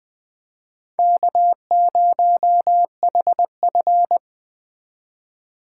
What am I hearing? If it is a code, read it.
Morse code, heard as a keyed tone, about 20 wpm, 700 Hz. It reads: K0HF
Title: Speed Racing - USA 1x2 Call Signs 17wpm